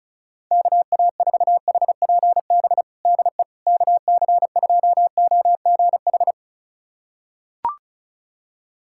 Morse code, heard as a keyed tone, about 36 wpm, 700 Hz. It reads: KA4HPB DE KC2OGH
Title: USA Callsign DE Callsign 35wpm